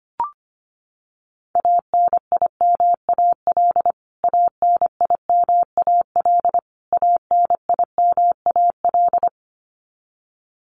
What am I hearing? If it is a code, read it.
Morse code, heard as a keyed tone, about 25 wpm, 700 Hz. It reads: ANIMAL ANIMAL ANIMAL
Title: Top 200 Words Repeated 25wpm